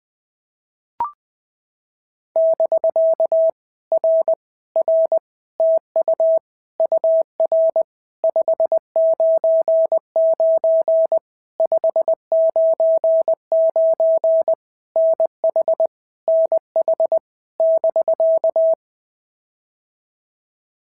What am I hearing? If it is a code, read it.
Morse code, heard as a keyed tone, about 20 wpm, 650 Hz. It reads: <BK> R R TU UR 599 599 NH NH <BK>
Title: POTA QSO Practice - 20wpm